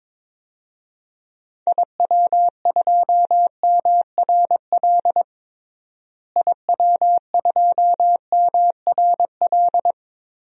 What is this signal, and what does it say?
Morse code, heard as a keyed tone, about 22 wpm, 700 Hz. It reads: IW2MRL IW2MRL
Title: DXCC-100 Call Signs - 6 Characters - Repeated 3x - 22wpm